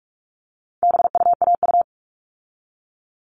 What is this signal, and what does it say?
Morse code, heard as a keyed tone, about 44 wpm, 700 Hz. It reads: BUAU
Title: All Cut Numbers - 4-Digit 45wpm